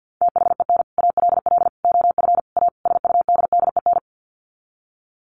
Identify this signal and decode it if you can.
Morse code, heard as a keyed tone, about 48 wpm, 700 Hz. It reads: THEN ALL OF A SUDDEN
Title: Sets of 5 Words Encore 50wpm